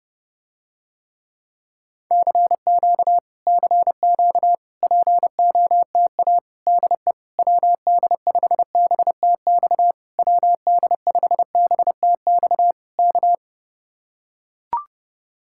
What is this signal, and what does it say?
Morse code, heard as a keyed tone, about 31 wpm, 700 Hz. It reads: CQ CQ POTA DE WD5BTX WD5BTX K